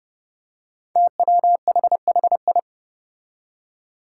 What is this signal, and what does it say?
Morse code, heard as a keyed tone, about 30 wpm, 700 Hz. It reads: TWHHI
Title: Five Challenging Letter-Number Combinations 30wpm